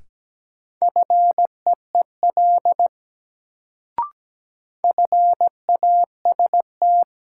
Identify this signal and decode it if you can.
Morse code, heard as a keyed tone, about 17 wpm, 700 Hz. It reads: FEEL FAST
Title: Sentences from Top 400 Words - New Word Drill - 17wpm